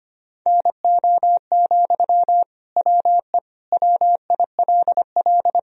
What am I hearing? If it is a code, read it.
Morse code, heard as a keyed tone, about 25 wpm, 700 Hz. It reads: NO, WE WILL
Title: Sentences from Top 1000 Words - Review Part 3 - 25wpm